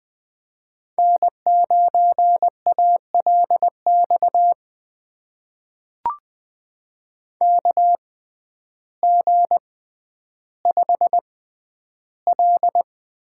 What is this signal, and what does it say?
Morse code, heard as a keyed tone, about 20 wpm, spaced as for 11 wpm, 700 Hz. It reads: N9ALX KG5L
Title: USA Call Signs - 5 Characters - 6x Character Spacing - 20wpm